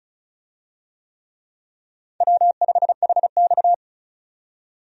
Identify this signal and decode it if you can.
Morse code, heard as a keyed tone, about 36 wpm, 700 Hz. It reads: W5HX
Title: USA Call Signs - 4 Characters - 35wpm